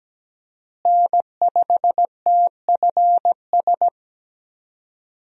Morse code, heard as a keyed tone, about 17 wpm, 700 Hz. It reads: N5TFS
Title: USA Call Signs - 5 Characters - Repeated 2x - 17wpm